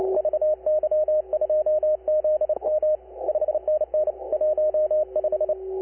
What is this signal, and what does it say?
Morse code, heard as a keyed tone, about 30 wpm, 560 Hz. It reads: VY2ZM 5NN 15
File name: VY2ZM pileup